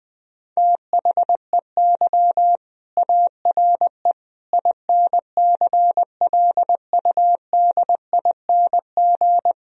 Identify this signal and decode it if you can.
Morse code, heard as a keyed tone, about 20 wpm, 700 Hz. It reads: THEY ARE INCLUDING
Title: Sentences from Top 1000 Words - Review Part 4 - 20wpm